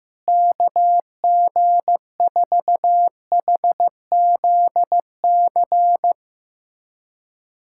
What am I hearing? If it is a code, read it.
Morse code, heard as a keyed tone, about 15 wpm, 700 Hz. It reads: KG4HZC
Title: USA Callsign DE Callsign - 6 Characters - 15wpm